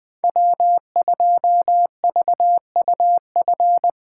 Morse code, heard as a keyed tone, about 21 wpm, 700 Hz. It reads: W2VUF
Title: K1USN SST QSO Practice - 20wpm